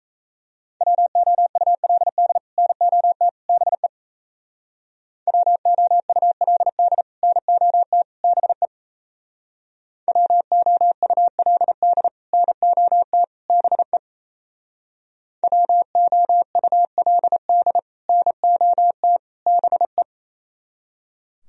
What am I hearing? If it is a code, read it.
Morse code, heard as a keyed tone, about 36 wpm, 700 Hz. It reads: WOULD NOT BE WOULD NOT BE WOULD NOT BE WOULD NOT BE